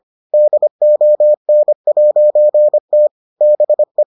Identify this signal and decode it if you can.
Morse code, heard as a keyed tone, about 25 wpm, 600 Hz. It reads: DON'T BE